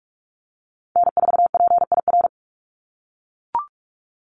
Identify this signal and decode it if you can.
Morse code, heard as a keyed tone, about 44 wpm, 700 Hz. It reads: N4PIR
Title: Call Signs - 45wpm